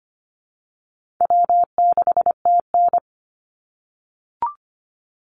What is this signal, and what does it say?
Morse code, heard as a keyed tone, about 25 wpm, 700 Hz. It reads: W6TN
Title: Easy Call Signs - 25wpm